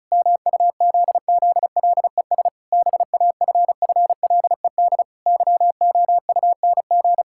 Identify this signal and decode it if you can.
Morse code, heard as a keyed tone, about 36 wpm, 700 Hz. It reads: MUZZLES BAFFLED YOUNG
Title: Pangram Challenge v2 - 35wpm